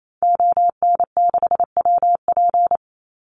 Morse code, heard as a keyed tone, about 28 wpm, 700 Hz. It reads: ON6WP
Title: Call Signs - 28wpm